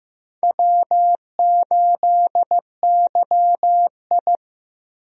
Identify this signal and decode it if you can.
Morse code, heard as a keyed tone, about 15 wpm, 700 Hz. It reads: W8YI
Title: USA Call Signs - 4 Characters - Repeated 3x - 15wpm